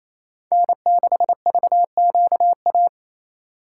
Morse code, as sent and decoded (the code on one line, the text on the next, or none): -. -.... ...- --.- .-
N6VQA